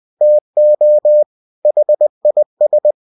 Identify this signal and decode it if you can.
Morse code, heard as a keyed tone, about 20 wpm, 600 Hz. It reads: TO HIS
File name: cw-books / winnie-the-pooh / winnie-20wpm-page-0020